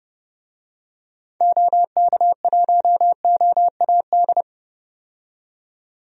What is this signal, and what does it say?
Morse code, heard as a keyed tone, about 30 wpm, 700 Hz. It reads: OK1OAD
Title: DXCC-100 Call Signs - 6 Characters - 30wpm